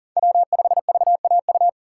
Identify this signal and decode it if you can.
Morse code, heard as a keyed tone, about 40 wpm, 700 Hz. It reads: W5VAU
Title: USA Call Signs - 6x Character Spacing - 40wpm